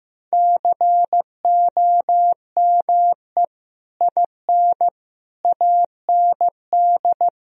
Morse code, heard as a keyed tone, about 15 wpm, 700 Hz. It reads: COME IN AND
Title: Sentences from Top 1000 Words - Review Part 1 - 15wpm